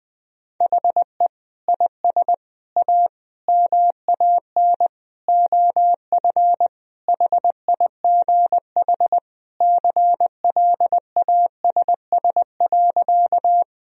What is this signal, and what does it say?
Morse code, heard as a keyed tone, about 21 wpm, 700 Hz. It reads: HE IS A MAN OF HIGH CLASS.
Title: Sentences from Top 1000 Words - Review Part 2 - 20wpm